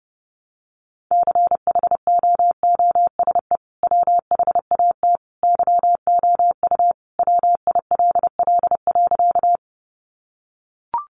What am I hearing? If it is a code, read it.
Morse code, heard as a keyed tone, about 28 wpm, 700 Hz. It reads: CHOOSE WHAT YOU WILL.
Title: Sentences from Top 700 Words 30wpm